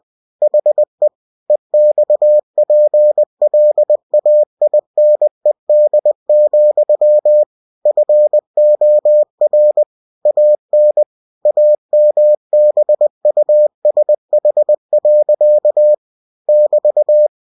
Morse code, heard as a keyed tone, about 20 wpm, 600 Hz. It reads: HE EXPLAINED, FOR AN AMBUSH. <BT>
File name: cw-books / winnie-the-pooh / winnie-20wpm-page-0082